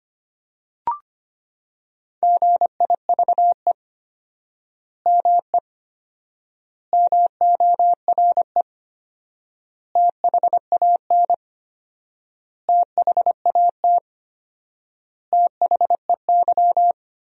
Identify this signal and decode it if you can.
Morse code, heard as a keyed tone, about 25 wpm, 700 Hz. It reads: GIVE ME MORE THAN THAT THEY
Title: 4x Word-Spacing - Sentences from Top 200 Words - 25wpm